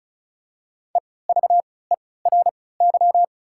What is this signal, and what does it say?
Morse code, heard as a keyed tone, about 36 wpm, spaced as for 20 wpm, 700 Hz. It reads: EVERY
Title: 3x Character Spacing - Top 500 Words - 35wpm